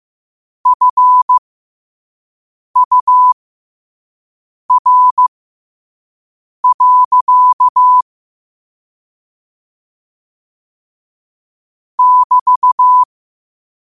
Morse code, heard as a keyed tone, about 15 wpm, spaced as for 5 wpm, 1000 Hz. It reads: FUR. <BT>
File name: qotdfiles / 5wpm